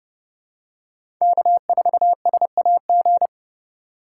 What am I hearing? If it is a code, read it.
Morse code, heard as a keyed tone, about 30 wpm, 700 Hz. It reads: K4SAG
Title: USA Call Signs - 5 Characters - 4x Character Spacing - 30wpm